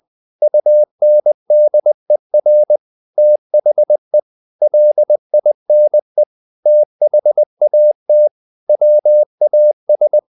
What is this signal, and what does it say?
Morse code, heard as a keyed tone, about 20 wpm, 600 Hz. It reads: UNDER THE LINE THAT WAS
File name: the old man and the sea / the old man and the sea part 1 20wpm